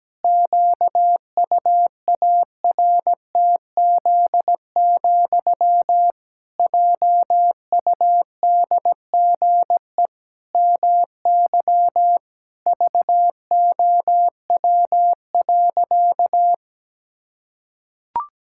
Morse code, heard as a keyed tone, about 17 wpm, 700 Hz. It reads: QUARTZ, JUDGE MY VOW.
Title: Pangram Challenge v2 - 17wpm